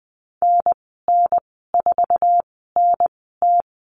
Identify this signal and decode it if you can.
Morse code, heard as a keyed tone, about 19 wpm, spaced as for 14 wpm, 700 Hz. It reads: NN4NT
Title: 179 Lesson 29g - Slash - Call Signs 20wpm